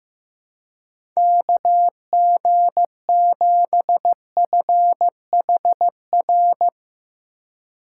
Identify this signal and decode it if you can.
Morse code, heard as a keyed tone, about 15 wpm, 700 Hz. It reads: KG7FHR
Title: USA Call Signs - 3x Character Spacing - 15wpm